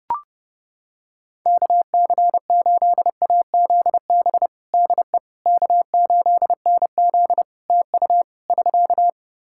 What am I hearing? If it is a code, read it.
Morse code, heard as a keyed tone, about 29 wpm, 700 Hz. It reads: KC8AZB DE K8NZ TU <SK>
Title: Practice QSO - Exchange 4 - Ending - 30wpm